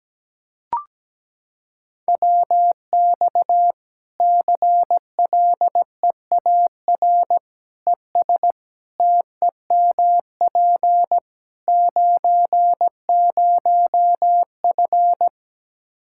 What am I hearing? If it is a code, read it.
Morse code, heard as a keyed tone, about 17 wpm, 700 Hz. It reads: WX CLEAR ES TEMP 90F
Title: Practice QSO - Exchange 2 - Info - 17wpm